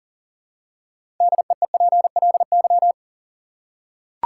Morse code, heard as a keyed tone, about 38 wpm, 700 Hz. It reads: DEEPLY
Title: Six Letter Words - 40wpm